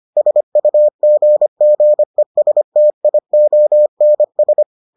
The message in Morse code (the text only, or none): SUGGESTIONS